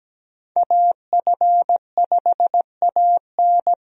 Morse code, heard as a keyed tone, about 17 wpm, 700 Hz. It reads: AF5AN
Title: USA Call Signs - 5 Characters - 17wpm